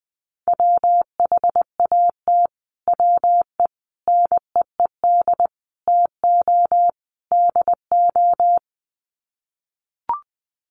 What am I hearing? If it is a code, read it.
Morse code, heard as a keyed tone, about 20 wpm, 700 Hz. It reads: WHAT WE NEED TO DO
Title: Sets of 5 Words Encore 20wpm